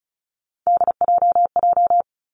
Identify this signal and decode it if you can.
Morse code, heard as a keyed tone, about 36 wpm, 700 Hz. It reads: DJJ
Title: Three Challenging Letter-Number Combinations 35wpm